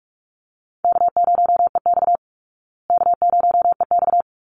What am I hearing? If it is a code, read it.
Morse code, heard as a keyed tone, about 44 wpm, 700 Hz. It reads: K0EX K0EX
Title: Easy Call Signs Repeated 2x - 45wpm